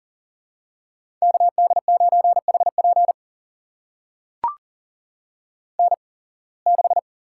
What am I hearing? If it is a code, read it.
Morse code, heard as a keyed tone, about 39 wpm, spaced as for 26 wpm, 700 Hz. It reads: KD9HP N6
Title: USA Call Signs - 5 Characters - 8x Character Spacing - 40wpm